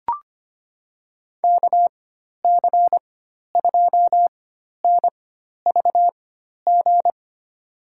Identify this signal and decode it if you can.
Morse code, heard as a keyed tone, about 25 wpm, spaced as for 12 wpm, 700 Hz. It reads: KC2NVG
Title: USA Call Signs - 4x Character Spacing - 25wpm